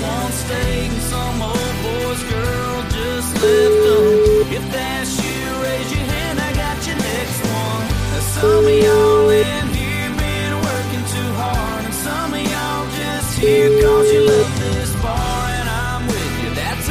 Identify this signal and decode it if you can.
Ringback tone (Europe, 425 Hz), with noise.